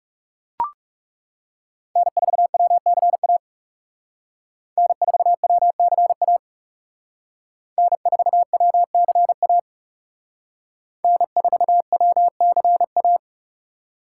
Morse code, heard as a keyed tone, about 39 wpm, 700 Hz. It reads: N4WCA N4WCA N4WCA N4WCA